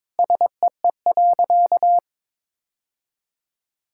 Morse code, heard as a keyed tone, about 22 wpm, 700 Hz. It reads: SEE.
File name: practice / 22wpm / Sentences with 3-Word Phrases - 22wpm